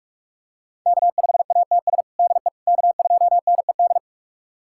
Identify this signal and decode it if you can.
Morse code, heard as a keyed tone, about 46 wpm, 700 Hz. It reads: K5ATS DE K2NED
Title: USA Callsign DE Callsign - 5 Characters - 45wpm